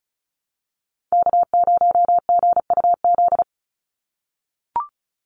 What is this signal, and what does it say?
Morse code, heard as a keyed tone, about 34 wpm, 700 Hz. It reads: K0GUZ
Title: Call Signs - 35wpm